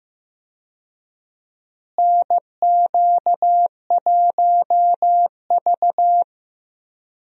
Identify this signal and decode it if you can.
Morse code, heard as a keyed tone, about 15 wpm, 700 Hz. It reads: NQ1V
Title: USA Call Signs - 4 Characters - Repeated 3x - 15wpm